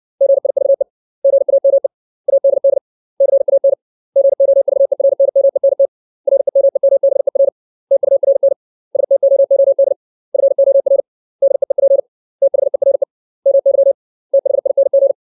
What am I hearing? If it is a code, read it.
Morse code, heard as a keyed tone, about 60 wpm, 550 Hz. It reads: GIVE GAME ADD CAN GOVERNMENT REMEMBER TURN STOOD FOR DEEP THERE MY THING
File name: 632 words / 632 Words 60WPM